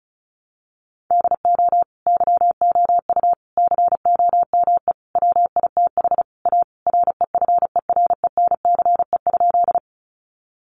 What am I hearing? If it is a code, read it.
Morse code, heard as a keyed tone, about 34 wpm, 700 Hz. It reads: DO YOU COME WITH A REFERENCE?
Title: Sentences from Top 1200 Words 35wpm